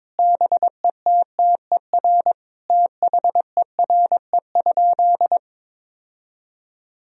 Morse code, heard as a keyed tone, about 22 wpm, 700 Hz. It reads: BETTER THERE?
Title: Sentences from Top 1000 Words - Review Part 4 - 22wpm